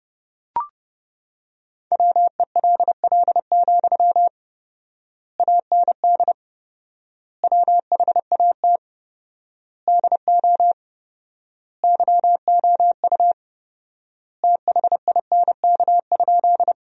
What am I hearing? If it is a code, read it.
Morse code, heard as a keyed tone, about 30 wpm, 700 Hz. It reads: WELL, AND WHAT DO YOU THINK?
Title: 4x Word-Spacing - Sentences from Top 200 Words - 30wpm